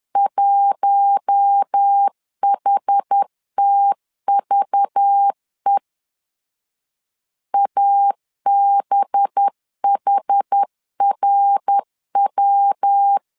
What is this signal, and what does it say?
Morse code, heard as a keyed tone, about 10 wpm, 800 Hz. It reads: 1HTVE ABHRW
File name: LEZIONI / 6 PKX0 / 6 PKX0 10wpm 2707